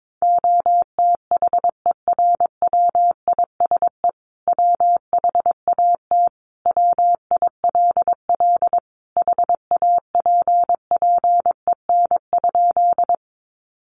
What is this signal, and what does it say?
Morse code, heard as a keyed tone, about 22 wpm, 700 Hz. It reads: OTHERWISE WHAT WILL HAPPEN?
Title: Sentences from Top 1200 Words 22wpm